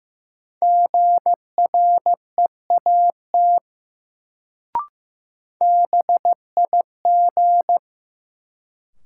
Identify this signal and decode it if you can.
Morse code, heard as a keyed tone, about 15 wpm, 700 Hz. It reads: GREAT BIG
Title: Sets of 2 Words - New Word Drill - 15wpm